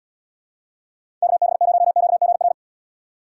Morse code, heard as a keyed tone, about 77 wpm, 700 Hz. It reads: KB0QGR